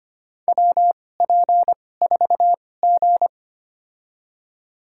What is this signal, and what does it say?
Morse code, heard as a keyed tone, about 25 wpm, spaced as for 18 wpm, 700 Hz. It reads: WP4G
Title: USA Call Signs - 4 Characters - 2x Character Spacing - 25wpm